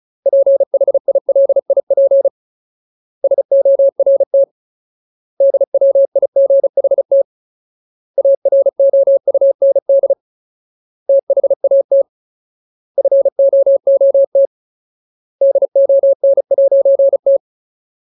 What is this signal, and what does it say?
Morse code, heard as a keyed tone, about 34 wpm, 550 Hz. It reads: PHILIP SORT DWIGHT AROUND THAT FOOT DON'T
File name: Ultimate Word List / 35WPM Ultimate